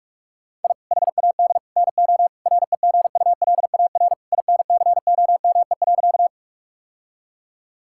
Morse code, heard as a keyed tone, about 41 wpm, 700 Hz. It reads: I HAD NO REGULAR INCOME.
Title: Sentences from Top 1000 Words - Review Part 6 - 45wpm